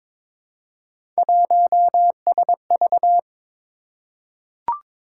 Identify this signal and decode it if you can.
Morse code, heard as a keyed tone, about 22 wpm, 700 Hz. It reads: 1SV
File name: practice / 22wpm / Three Letter-Number Combinations - 22wpm